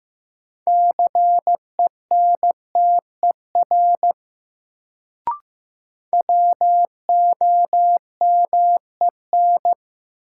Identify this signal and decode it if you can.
Morse code, heard as a keyed tone, about 15 wpm, 700 Hz. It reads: CENTER WOMEN
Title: Sets of 3 Words - New Word Drill - 15wpm